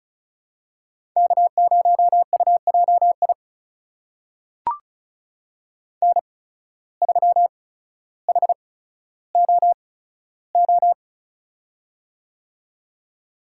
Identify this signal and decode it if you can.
Morse code, heard as a keyed tone, about 36 wpm, spaced as for 15 wpm, 700 Hz. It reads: K0UJI N3HOO
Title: USA Call Signs - 5 Characters - 8x Character Spacing - 35wpm